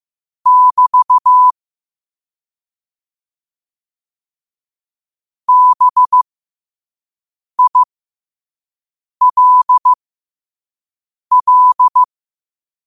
Morse code, heard as a keyed tone, about 15 wpm, spaced as for 5 wpm, 1000 Hz. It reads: <BT> BILL
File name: qotdfiles / 5wpm